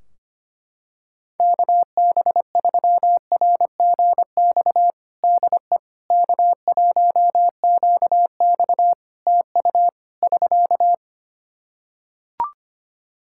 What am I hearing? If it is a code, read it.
Morse code, heard as a keyed tone, about 25 wpm, 700 Hz. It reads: KB3RGX DE K1QX TU <SK>
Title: Practice QSO - Exchange 4 - Ending - 25wpm